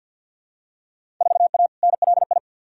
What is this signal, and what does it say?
Morse code, heard as a keyed tone, about 52 wpm, 700 Hz. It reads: 4A NLI
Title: Field Day Exchanges Encore - 50wpm